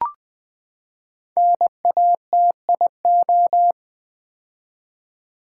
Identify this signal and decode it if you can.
Morse code, heard as a keyed tone, about 20 wpm, 700 Hz. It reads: NATIO
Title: Build A Word - 5 Letters At A Time - 10-Letter Words - 20wpm